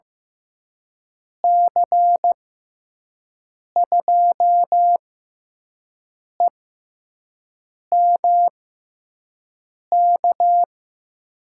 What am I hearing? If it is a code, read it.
Morse code, heard as a keyed tone, about 15 wpm, spaced as for 5 wpm, 700 Hz. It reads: C2EMK
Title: USA Call Signs - 6 Characters - 6x Character Spacing - 15wpm